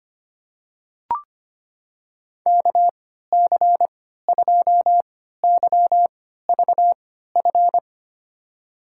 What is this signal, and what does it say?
Morse code, heard as a keyed tone, about 25 wpm, spaced as for 14 wpm, 700 Hz. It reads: KC2YVF
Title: USA Call Signs - 6 Characters - 3x Character Spacing - 25wpm